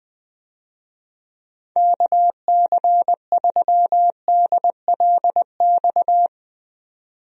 Morse code, heard as a keyed tone, about 20 wpm, 700 Hz. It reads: KC3DLX
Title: USA Call Signs - 6 Characters - 20wpm